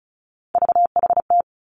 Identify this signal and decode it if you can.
Morse code, heard as a keyed tone, about 36 wpm, 700 Hz. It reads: VHT
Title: Three Challenging Letter-Number Combinations 35wpm